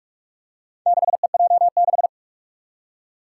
Morse code, heard as a keyed tone, about 46 wpm, 700 Hz. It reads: 6EJ6
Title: Four Challenging Letter-Number Combinations 45wpm